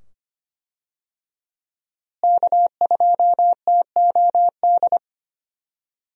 Morse code, heard as a keyed tone, about 25 wpm, 700 Hz. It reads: K2TOD
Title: USA Call Signs - 5 Characters - Repeated 2x - 25wpm